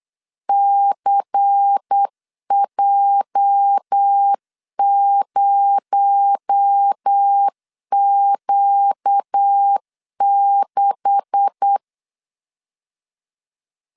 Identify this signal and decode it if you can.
Morse code, heard as a keyed tone, about 8 wpm, 800 Hz. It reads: CJ0Q6
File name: LEZIONI / 9 286 8wpm 3425